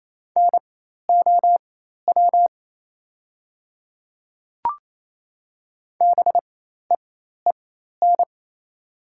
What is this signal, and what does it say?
Morse code, heard as a keyed tone, about 28 wpm, spaced as for 13 wpm, 700 Hz. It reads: NOW BEEN
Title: 4x Character Spacing - Top 100 Words - 28wpm